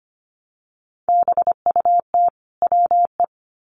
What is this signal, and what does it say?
Morse code, heard as a keyed tone, about 25 wpm, 700 Hz. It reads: BUT WE
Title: Sets of 2 Words 25wpm